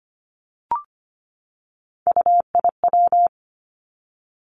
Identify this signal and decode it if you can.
Morse code, heard as a keyed tone, about 25 wpm, 700 Hz. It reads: UIW
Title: Three Challenging Letter-Number Combinations 25wpm